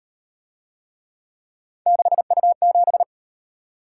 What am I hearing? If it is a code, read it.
Morse code, heard as a keyed tone, about 38 wpm, 700 Hz. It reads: 6U7